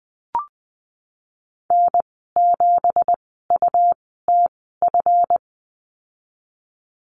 Morse code, heard as a keyed tone, about 19 wpm, spaced as for 14 wpm, 700 Hz. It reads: N7UTF
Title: 179 Lesson 29g - Slash - Call Signs 20wpm